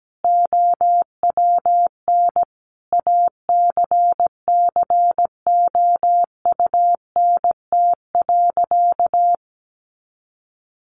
Morse code, heard as a keyed tone, about 17 wpm, 700 Hz. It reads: OWN ACCOUNT.
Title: Sentences from Top 1400 Words 17wpm